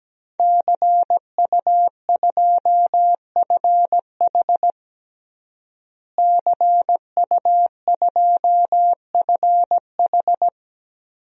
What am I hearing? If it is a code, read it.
Morse code, heard as a keyed tone, about 17 wpm, 700 Hz. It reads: CU2FH CU2FH
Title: DXCC-100 Call Signs - 5 Characters - Repeated 3x - 17wpm